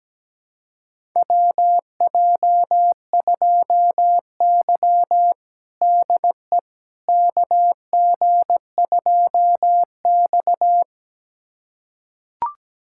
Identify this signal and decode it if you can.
Morse code, heard as a keyed tone, about 17 wpm, 700 Hz. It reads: WJ2Y DE KG2X
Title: USA Callsign DE Callsign - 4 Characters - 17wpm